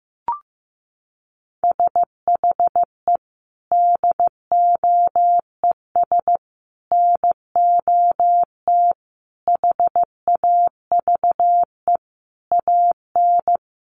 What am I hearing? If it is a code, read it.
Morse code, heard as a keyed tone, about 15 wpm, 700 Hz. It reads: SHE DOES NOT HAVE AN
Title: Sentences from Top 1200 Words 15wpm